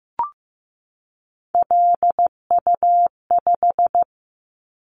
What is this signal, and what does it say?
Morse code, heard as a keyed tone, about 15 wpm, 700 Hz. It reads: LU5
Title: Callsign Prefixes for Top 50 DXCC - 15wpm